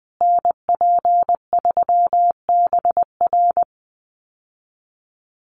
Morse code, heard as a keyed tone, about 20 wpm, 700 Hz. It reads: NP3BR
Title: USA Call Signs Repeated 2x 20wpm